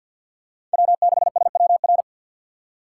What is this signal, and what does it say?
Morse code, heard as a keyed tone, about 52 wpm, 700 Hz. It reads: W6SWR
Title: USA Call Signs - 5 Characters - 6x Character Spacing - 50wpm